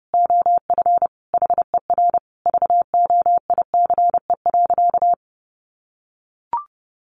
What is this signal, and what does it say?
Morse code, heard as a keyed tone, about 30 wpm, 700 Hz. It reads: OF HER VOICE.
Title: Sentences from Top 1100 Words 30wpm